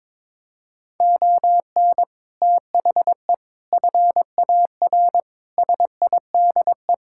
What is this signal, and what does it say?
Morse code, heard as a keyed tone, about 22 wpm, 700 Hz. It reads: ON THE FAR SIDE
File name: practice / 22wpm / Sets of 6 Words - 22wpm